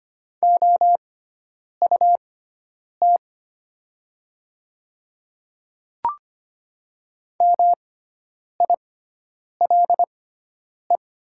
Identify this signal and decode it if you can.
Morse code, heard as a keyed tone, about 25 wpm, spaced as for 9 wpm, 690 Hz. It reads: OUT MILE
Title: 6x Character Spacing - Top 300 Words - 25wpm